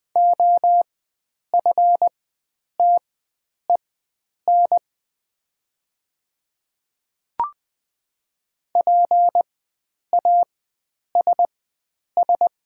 Morse code, heard as a keyed tone, about 20 wpm, spaced as for 9 wpm, 700 Hz. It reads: OFTEN PASS
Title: 4x Character Spacing - Top 500 Words - 20wpm